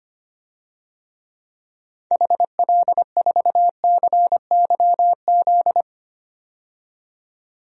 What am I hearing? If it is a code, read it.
Morse code, heard as a keyed tone, about 25 wpm, 700 Hz. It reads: HL4CYZ
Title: DXCC-100 Call Signs - 6 Characters - 25wpm